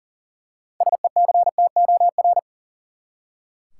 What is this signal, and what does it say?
Morse code, heard as a keyed tone, about 40 wpm, 700 Hz. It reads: SECTOR
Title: Six Letter Words - 40wpm